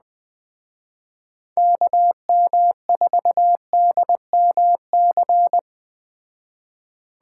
Morse code, heard as a keyed tone, about 21 wpm, 700 Hz. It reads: KM4DMC